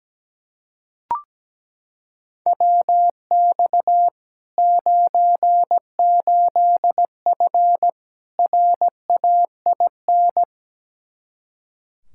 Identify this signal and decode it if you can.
Morse code, heard as a keyed tone, about 17 wpm, 700 Hz. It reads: WX 98F RAIN
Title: Practice QSO - Exchange 2 - Info - 17wpm